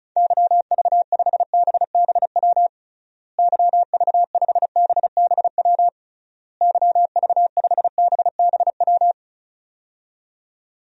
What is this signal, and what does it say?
Morse code, heard as a keyed tone, about 36 wpm, 700 Hz. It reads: YV5BBW YV5BBW YV5BBW
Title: DXCC-100 Call Signs - 6 Characters - Repeated 3x - 35wpm